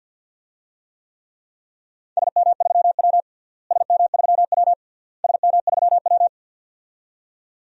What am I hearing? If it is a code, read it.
Morse code, heard as a keyed tone, about 50 wpm, 700 Hz. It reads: SM3W SM3W SM3W
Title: DXCC-100 Call Signs - 4 Characters - Repeated 3x - 50wpm